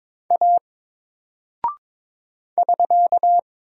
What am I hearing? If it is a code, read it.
Morse code, heard as a keyed tone, about 22 wpm, 700 Hz. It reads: A <SK>
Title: Single Character (Letter:Number:Punctuation:Prosign) Alt 22wpm